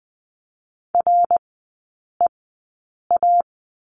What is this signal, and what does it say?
Morse code, heard as a keyed tone, about 20 wpm, spaced as for 8 wpm, 700 Hz. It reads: REA
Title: 053 Lesson 10e - Letter r - Three Characters 20wpm